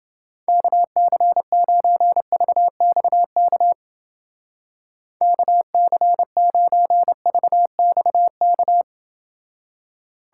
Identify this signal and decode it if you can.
Morse code, heard as a keyed tone, about 30 wpm, 700 Hz. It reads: KC9VXK KC9VXK